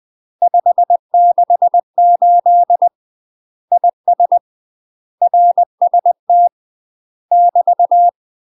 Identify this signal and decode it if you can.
Morse code, heard as a keyed tone, about 20 wpm, 700 Hz. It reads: 568 IS RST <BT>
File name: UK MASTER QSOs TRAINING / 20WPM / UK MASTER QSOs 0061